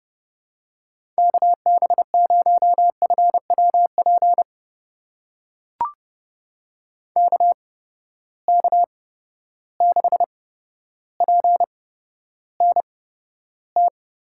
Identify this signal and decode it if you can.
Morse code, heard as a keyed tone, about 32 wpm, spaced as for 13 wpm, 700 Hz. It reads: KB0FWP KK6PNT
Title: USA Call Signs - 6 Characters - 8x Character Spacing - 30wpm